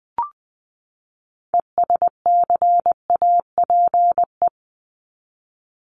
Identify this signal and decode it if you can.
Morse code, heard as a keyed tone, about 20 wpm, 700 Hz. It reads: ESCAPE
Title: Top 1501-1750 Words 20wpm